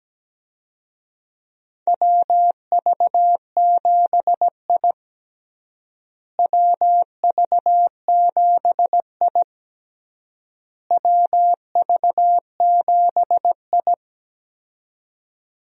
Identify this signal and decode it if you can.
Morse code, heard as a keyed tone, about 17 wpm, 700 Hz. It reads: WV7I WV7I WV7I
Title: USA Call Signs - 4 Characters - Repeated 3x - 17wpm